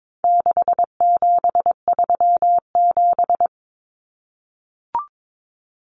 Morse code, heard as a keyed tone, about 22 wpm, 700 Hz. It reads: 6737
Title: Common Cut Numbers - 4-Digit 22wpm